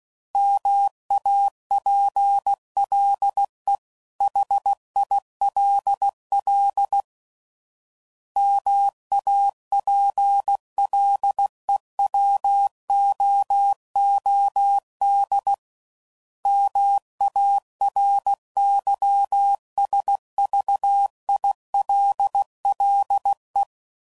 Morse code, heard as a keyed tone, about 16 wpm, 800 Hz. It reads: MAPLE HILL MAPLEWOOD MARYSVILLE
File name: CW Code Groups @15WPM / fromSM's / Cities